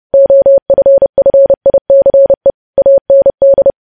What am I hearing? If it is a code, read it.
Morse code, heard as a keyed tone, about 30 wpm, 550 Hz. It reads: OFFICE AND
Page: news headlines in morse code at 30 wpm